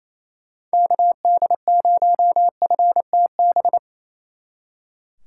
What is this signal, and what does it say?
Morse code, heard as a keyed tone, about 33 wpm, 700 Hz. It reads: KD0FTB
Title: Speed Racing - USA 2x3 Call Signs 28wpm